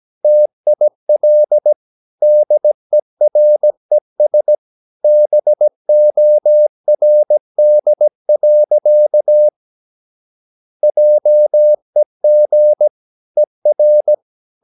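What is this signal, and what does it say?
Morse code, heard as a keyed tone, about 17 wpm, 600 Hz. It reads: TIL DERES BORD. JEG ER
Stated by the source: Slow (QRS) Morse practice at 17 wpm, from CW med Gnister.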